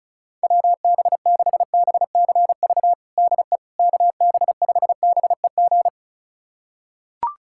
Morse code, heard as a keyed tone, about 36 wpm, 700 Hz. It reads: WB6BCV DE KB5BEG
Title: USA Callsign DE Callsign - 6 Characters - 35wpm